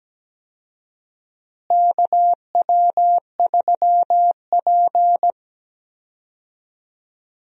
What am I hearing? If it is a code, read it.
Morse code, heard as a keyed tone, about 17 wpm, 700 Hz. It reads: KW3P
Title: USA Call Signs - 4 Characters - Repeated 3x - 17wpm